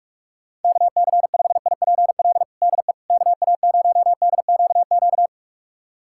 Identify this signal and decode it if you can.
Morse code, heard as a keyed tone, about 41 wpm, 700 Hz. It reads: KC5IPL DE KA0DQQ
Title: USA Callsign DE Callsign - 6 Characters - 45wpm